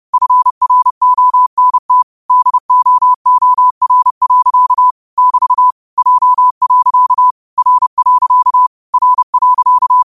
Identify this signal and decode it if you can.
Morse code, heard as a keyed tone, about 30 wpm, 1000 Hz. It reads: FRONT DOOR. <BT> J. R. R.
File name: qotdfiles / 30wpm